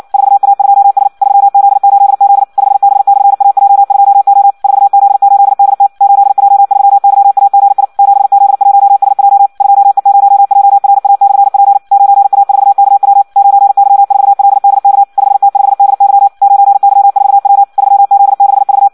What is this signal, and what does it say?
Morse code, heard as a keyed tone, about 56 wpm, 800 Hz. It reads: 4A2S 3Z8Z 5B/AJ2O 4Z8DT 8J3YAGI 7Z1SJ 2E0JRA/P 9A4KW 9Y4LDK 5T5DY 9Y4W 3Z6V
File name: Audio practica CW / qrq-tx-58 2